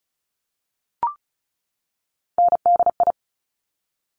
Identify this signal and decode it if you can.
Morse code, heard as a keyed tone, about 36 wpm, 700 Hz. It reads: NDI
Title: Top Three Letter Combinations in Words 35wpm